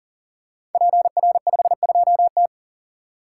Morse code, heard as a keyed tone, about 40 wpm, 700 Hz. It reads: PR52T
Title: Code Groups - 5 Letter:Numbers - 40wpm